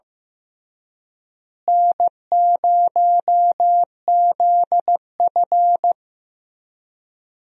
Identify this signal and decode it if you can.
Morse code, heard as a keyed tone, about 15 wpm, 700 Hz. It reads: N0ZF
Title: USA Call Signs - 4 Characters - Repeated 3x - 15wpm